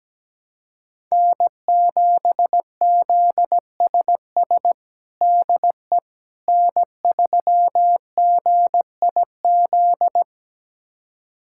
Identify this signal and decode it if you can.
Morse code, heard as a keyed tone, about 17 wpm, 700 Hz. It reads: N7ZSS DE N3GIZ
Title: USA Callsign DE Callsign - 5 Characters - 17wpm